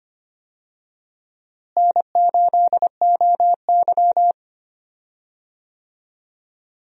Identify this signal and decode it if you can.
Morse code, heard as a keyed tone, about 25 wpm, 700 Hz. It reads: N8OY
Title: USA Call Signs - 4 Characters - Repeated 2x - 25wpm